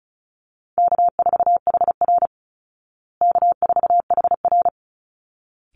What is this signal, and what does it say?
Morse code, heard as a keyed tone, about 34 wpm, 700 Hz. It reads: K4HR K4HR
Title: Easy Call Signs Repeated 2x - 35wpm